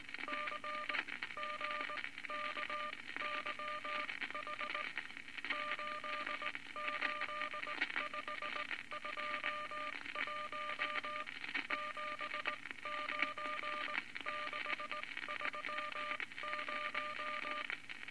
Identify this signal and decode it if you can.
Morse code, heard as a keyed tone, about 19 wpm, spaced as for 15 wpm, 1200 Hz, with noise. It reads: CGKYH 88521 79630
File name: Morse Code Records / 19 B LCM 167 Twelve Words Per Minute